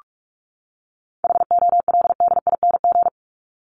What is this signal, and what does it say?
Morse code, heard as a keyed tone, about 44 wpm, 700 Hz. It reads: HOLDING
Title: ING Suffix - Top 100 Words with Base Words 45wpm